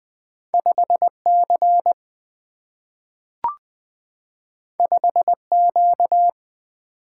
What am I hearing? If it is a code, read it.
Morse code, heard as a keyed tone, about 20 wpm, 700 Hz. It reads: 5C 5Q
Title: Single-Letter Single-Number 20wpm